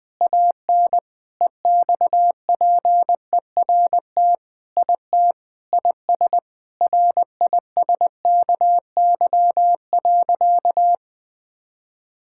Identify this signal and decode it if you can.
Morse code, heard as a keyed tone, about 21 wpm, 700 Hz. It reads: AN EXPERT IT IS RISKY.
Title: Sentences with 6-Word Phrases - 20wpm